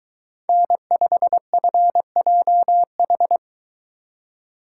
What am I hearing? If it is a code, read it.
Morse code, heard as a keyed tone, about 20 wpm, 700 Hz. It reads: N5FJH